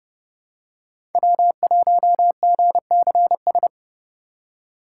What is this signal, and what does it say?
Morse code, heard as a keyed tone, about 30 wpm, 700 Hz. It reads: W1GCS
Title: K1USN SST QSO Practice - 30wpm